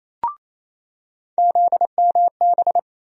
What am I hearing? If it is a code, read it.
Morse code, heard as a keyed tone, about 28 wpm, 700 Hz. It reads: ZMB
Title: USA Call Sign Postfixes - 3 Letters 28wpm